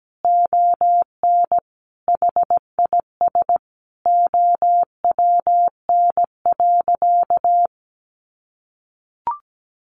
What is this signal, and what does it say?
Morse code, heard as a keyed tone, about 17 wpm, 700 Hz. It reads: ON HIS OWN.
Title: Sentences from Top 1400 Words 17wpm